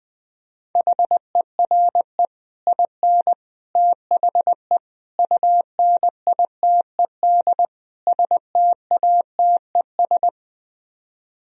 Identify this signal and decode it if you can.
Morse code, heard as a keyed tone, about 21 wpm, 700 Hz. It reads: HERE IN THE UNITED STATES